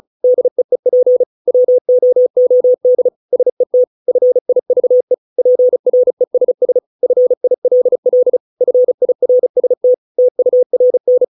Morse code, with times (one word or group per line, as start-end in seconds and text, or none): 0.2-1.2 DEEP
1.5-3.1 WOOD
3.3-3.8 SET
4.1-5.1 FIVE
5.4-6.8 PRESS
7.0-8.4 FILL
8.6-9.9 FIRST
10.2-11.2 TURN